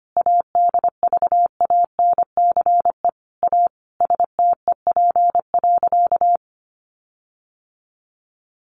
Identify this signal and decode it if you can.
Morse code, heard as a keyed tone, about 25 wpm, 700 Hz. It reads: ADVANCE A STEP.
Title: Sentences from Top 1400 Words 25wpm